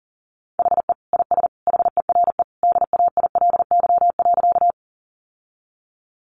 Morse code, heard as a keyed tone, about 40 wpm, 700 Hz. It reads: HE IS HERE DAILY.
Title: Sentences from Top 1100 Words 40wpm